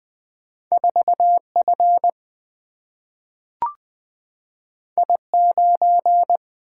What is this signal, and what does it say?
Morse code, heard as a keyed tone, about 20 wpm, 700 Hz. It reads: 4F I9
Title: Single-Letter Single-Number 20wpm